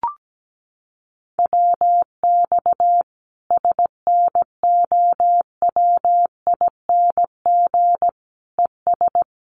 Morse code, heard as a keyed tone, about 17 wpm, 700 Hz. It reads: WX SNOWING ES
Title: Common QSO Phrases 17wpm